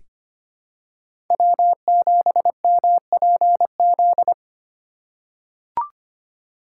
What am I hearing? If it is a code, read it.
Morse code, heard as a keyed tone, about 25 wpm, 700 Hz. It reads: W7MPZ
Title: USA Call Signs - 5 Characters - 3x Character Spacing - 25wpm